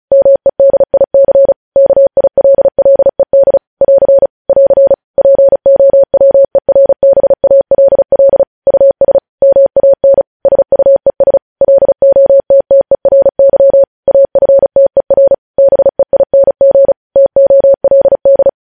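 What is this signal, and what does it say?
Morse code, heard as a keyed tone, about 36 wpm, 550 Hz. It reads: MEDIC KILLED <AR> <AR> POWERBALL US MAN SUES LOTTERY AFTER BEING TOLD
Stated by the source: Morse news headlines practice at 35.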